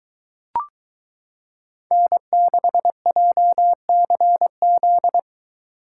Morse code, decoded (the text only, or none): N6JCZ